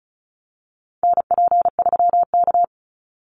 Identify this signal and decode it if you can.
Morse code, heard as a keyed tone, about 34 wpm, 700 Hz. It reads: NP3K
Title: Call Signs - 35wpm